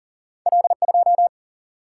Morse code, heard as a keyed tone, about 40 wpm, 700 Hz. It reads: L 2
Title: Single-Letter Single-Number 40wpm